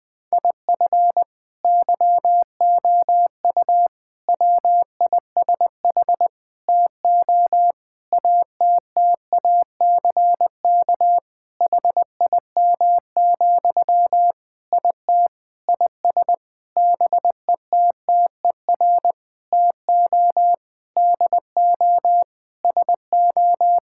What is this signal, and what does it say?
Morse code, heard as a keyed tone, about 20 wpm, 700 Hz. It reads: IF YOU WISH TO ATTACK HIM, IT IS BETTER TO DO SO
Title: Sentences from Top 1600 Words 20wpm